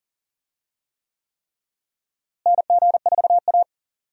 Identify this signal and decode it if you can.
Morse code, heard as a keyed tone, about 40 wpm, 700 Hz. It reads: NG4A
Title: USA Call Signs - 4 Characters - Repeated 3x - 40wpm